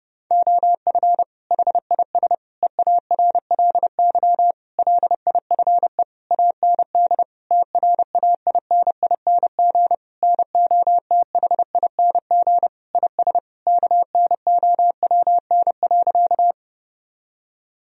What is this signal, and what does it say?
Morse code, heard as a keyed tone, about 32 wpm, 700 Hz. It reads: OF HIS EARLY LIFE AND TRAINING NOTHING IS KNOWN.
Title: Sentences from Top 1000 Words - Review Part 4 - 30wpm